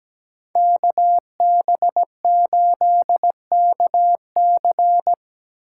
Morse code, heard as a keyed tone, about 20 wpm, 700 Hz. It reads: KB8KC
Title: Speed Racing - USA 2x2 Call Signs 17wpm